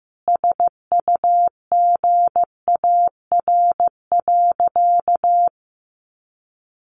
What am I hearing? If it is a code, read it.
Morse code, heard as a keyed tone, about 15 wpm, 700 Hz. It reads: SUGAR.
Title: Sentences from Top 1200 Words 15wpm